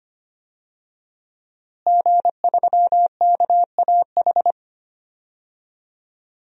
Morse code, heard as a keyed tone, about 25 wpm, 700 Hz. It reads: G3KAH
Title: DXCC-100 Call Signs - 5 Characters - Repeated 2x - 25wpm